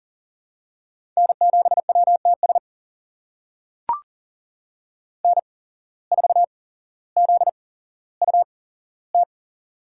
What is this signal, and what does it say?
Morse code, heard as a keyed tone, about 40 wpm, spaced as for 17 wpm, 700 Hz. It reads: N7WTS N4ZUT
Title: USA Call Signs - 5 Characters - 8x Character Spacing - 40wpm